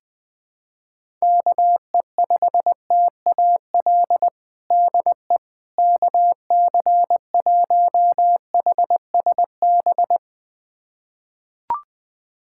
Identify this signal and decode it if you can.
Morse code, heard as a keyed tone, about 20 wpm, 700 Hz. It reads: KE5TAL DE KC1HSB
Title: USA Callsign DE Callsign - 6 Characters - 20wpm